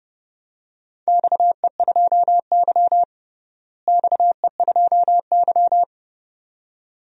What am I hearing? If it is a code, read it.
Morse code, heard as a keyed tone, about 30 wpm, 700 Hz. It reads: XE2Y XE2Y
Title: DXCC-100 Call Signs - 4 Characters - Repeated 2x - 30wpm